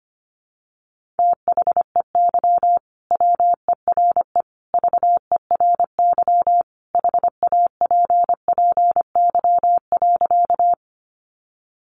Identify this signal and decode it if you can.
Morse code, heard as a keyed tone, about 25 wpm, 700 Hz. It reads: THEY WERE VERY HAPPY.
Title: Sentences from Top 1000 Words 25wpm